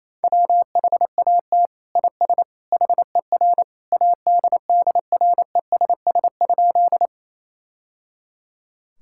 Morse code, heard as a keyed tone, about 28 wpm, 700 Hz. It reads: WHAT IS HER ADDRESS?
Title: Sentences from Top 1000 Words - Review Part 6 - 28wpm